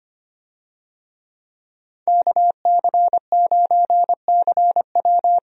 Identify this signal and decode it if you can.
Morse code, heard as a keyed tone, about 25 wpm, 700 Hz. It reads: KC9CW
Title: USA Call Signs - 5 Characters - 25wpm